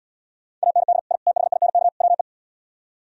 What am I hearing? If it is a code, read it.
Morse code, heard as a keyed tone, about 77 wpm, 700 Hz. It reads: AND I THINK WE